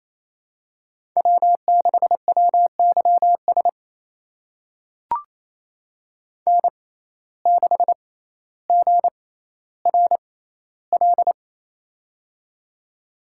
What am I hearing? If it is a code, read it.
Morse code, heard as a keyed tone, about 28 wpm, spaced as for 14 wpm, 700 Hz. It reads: W6WYS N6GRL
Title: USA Call Signs - 5 Characters - 6x Character Spacing - 28wpm